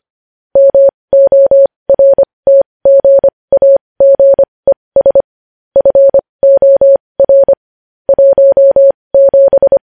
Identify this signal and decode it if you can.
Morse code, heard as a keyed tone, about 25 wpm, spaced as for 20 wpm, 550 Hz. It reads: MORTGAGES FOR 17
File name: podcast / News headlines-1x-25x20